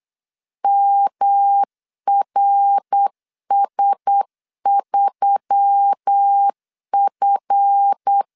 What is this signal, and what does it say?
Morse code, heard as a keyed tone, about 8 wpm, 800 Hz. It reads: MRS3F
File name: LEZIONI / 7 GF9Z / 7 GF 8wpm 3108